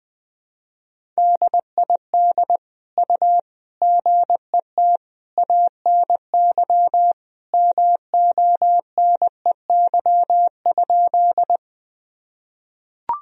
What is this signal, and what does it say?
Morse code, heard as a keyed tone, about 20 wpm, 700 Hz. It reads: DID U GET ANY MONEY?
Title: Sentences from Top 400 Words and Abbreviations 20wpm